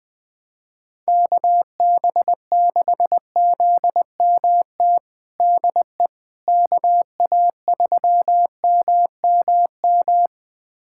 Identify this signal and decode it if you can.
Morse code, heard as a keyed tone, about 20 wpm, 700 Hz. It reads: KB6ZMT DE KA3MMM
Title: USA Callsign DE Callsign 20wpm